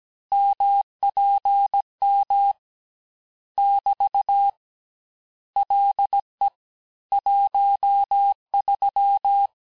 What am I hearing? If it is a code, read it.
Morse code, heard as a keyed tone, about 17 wpm, 800 Hz. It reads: MPM <BT> LE 13
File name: texte clair vitesse 16 puis 21 mpm